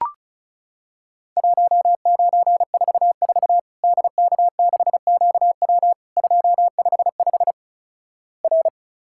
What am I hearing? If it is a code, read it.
Morse code, heard as a keyed tone, about 36 wpm, 700 Hz. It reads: 1944 DK6QW 255 R
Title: Worked All Europe DX Contest - QTC Exchange - 35wpm